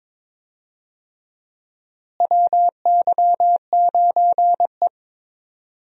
Morse code, heard as a keyed tone, about 22 wpm, 700 Hz. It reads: WY9E
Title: USA Call Signs - 4 Characters - Repeated 2x - 22wpm